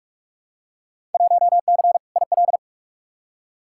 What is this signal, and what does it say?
Morse code, heard as a keyed tone, about 47 wpm, 700 Hz. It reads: 1C IL
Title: Field Day Exchanges - 45wpm